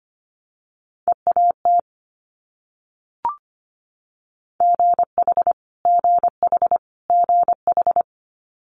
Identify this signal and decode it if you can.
Morse code, heard as a keyed tone, about 25 wpm, 700 Hz. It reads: EAT GH GH GH
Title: Multi-letter Phonemes Repeated 3x 25wpm